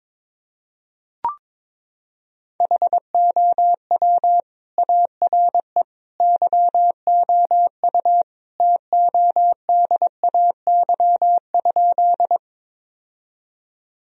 Morse code, heard as a keyed tone, about 22 wpm, 700 Hz. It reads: HOW ARE YOU TODAY?
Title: Sentences from Top 1000 Words - Review Part 4 - 22wpm